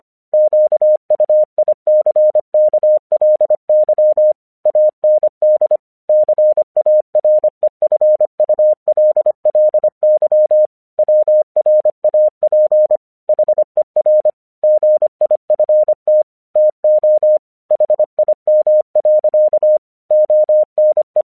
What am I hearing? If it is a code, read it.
Morse code, heard as a keyed tone, about 25 wpm, 610 Hz. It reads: QUICKLY AND CAREFULLY WRAP HER GIFT TO HIM. ONE
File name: 1 English 25wpm / AQC17